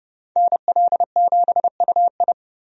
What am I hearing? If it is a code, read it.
Morse code, heard as a keyed tone, about 30 wpm, 700 Hz. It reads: NL7UI
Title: USA Call Signs - 5 Characters - Repeated 2x - 30wpm